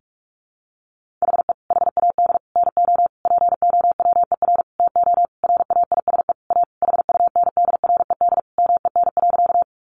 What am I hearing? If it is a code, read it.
Morse code, heard as a keyed tone, about 44 wpm, 700 Hz. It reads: HE HAD NO POWER TO RAISE A HUNDRED MEN.
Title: Sentences from Top 1400 Words 45wpm